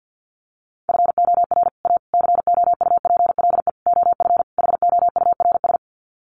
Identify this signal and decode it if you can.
Morse code, heard as a keyed tone, about 50 wpm, 700 Hz. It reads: FOR A COUPLE OF HOURS
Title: Sets of 5 Words Encore2 50wpm